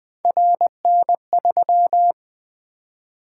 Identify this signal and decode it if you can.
Morse code, heard as a keyed tone, about 20 wpm, 700 Hz. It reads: RN3
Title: Three Letter-Number Combinations - 20wpm